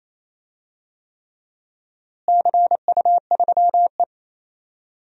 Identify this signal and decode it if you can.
Morse code, heard as a keyed tone, about 28 wpm, 700 Hz. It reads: CU3E
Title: DXCC-100 Call Signs - 4 Characters - Repeated 3x - 28wpm